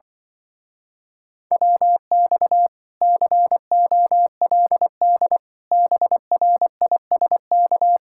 Morse code, heard as a keyed tone, about 24 wpm, 700 Hz. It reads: WX COLD BRISK
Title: Speed-Racing Common QSO Phrases - 20wpm